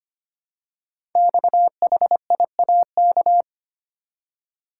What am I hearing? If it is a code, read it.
Morse code, heard as a keyed tone, about 25 wpm, 700 Hz. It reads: XHIAK
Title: Code Groups - 5 Letter:Numbers - 25wpm